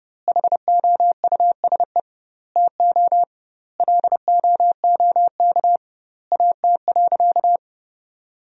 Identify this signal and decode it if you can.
Morse code, heard as a keyed tone, about 30 wpm, 700 Hz. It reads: HOUSE TO LOOK AT.
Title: 2x Word-Spacing - Sentences from Top 200 Words - 30wpm